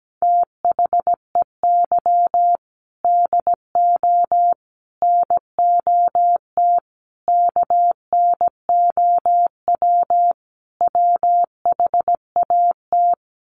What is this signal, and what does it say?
Morse code, heard as a keyed tone, about 17 wpm, 700 Hz. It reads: THEY DO NOT KNOW WHAT
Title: Sets of 5 Words Encore 17wpm